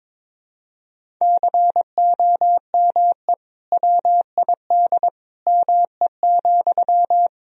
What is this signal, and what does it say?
Morse code, heard as a keyed tone, about 22 wpm, 700 Hz. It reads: COME WID ME,
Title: Sentences from Top 200 Words and Abbreviations 22wpm